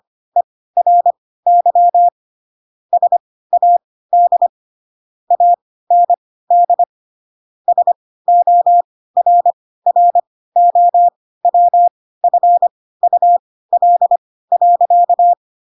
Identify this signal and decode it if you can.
Morse code, heard as a keyed tone, about 25 wpm, spaced as for 16 wpm, 700 Hz. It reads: ERY SAD AND SORROWFUL.
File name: Pinocchio / Pinocchio ch7 10 25 ik1hge